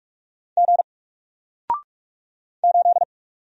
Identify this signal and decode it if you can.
Morse code, heard as a keyed tone, about 44 wpm, 700 Hz. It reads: G 8
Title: Single Letter-Number Alt 45wpm